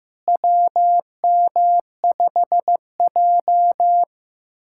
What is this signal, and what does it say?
Morse code, heard as a keyed tone, about 15 wpm, 700 Hz. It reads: WM5J
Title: USA Callsign DE Callsign - 4 Characters - 15wpm